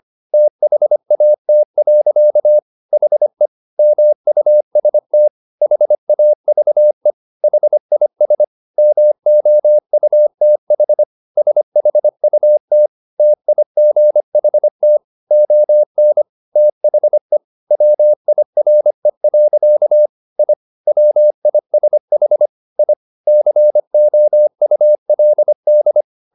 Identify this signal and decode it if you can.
Morse code, heard as a keyed tone, about 25 wpm, 600 Hz. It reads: THAT. HE MUST HAVE HIS MOUTH SHUT TIGHT ON THE WIRE. I WISH I COULD
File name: the old man and the sea part 1 25wpm